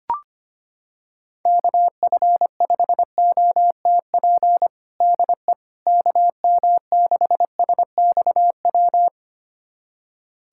Morse code, heard as a keyed tone, about 25 wpm, 700 Hz. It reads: KF5OTP DE KM6SXW
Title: USA Callsign DE Callsign 25wpm